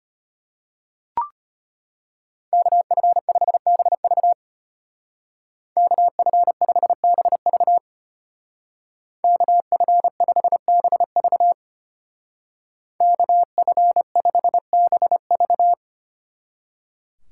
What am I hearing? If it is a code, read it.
Morse code, heard as a keyed tone, about 31 wpm, 700 Hz. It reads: KF5BV KF5BV KF5BV KF5BV
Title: Speed Racing - USA 2x2 Call Signs 25wpm